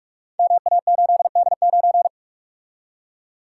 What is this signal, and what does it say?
Morse code, heard as a keyed tone, about 44 wpm, 700 Hz. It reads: MA8D9
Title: Code Groups - 5 Letter:Numbers - 45wpm